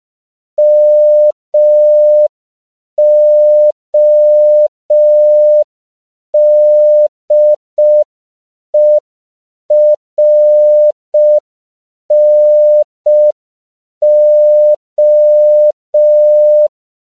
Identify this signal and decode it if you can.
Morse code, heard as a keyed tone, about 5 wpm, 600 Hz. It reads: MODERNO
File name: CW / 5PPM